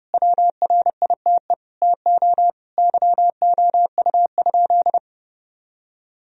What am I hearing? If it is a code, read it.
Morse code, heard as a keyed tone, about 32 wpm, 700 Hz. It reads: WRITE TO YOU?
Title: Sentences from Top 1000 Words - Review Part 3 - 30wpm